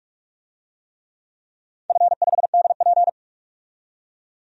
Morse code, heard as a keyed tone, about 45 wpm, 700 Hz. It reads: F5DP
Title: DXCC-100 Call Signs - 4 Characters - 45wpm